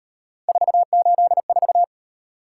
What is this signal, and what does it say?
Morse code, heard as a keyed tone, about 38 wpm, 700 Hz. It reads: 484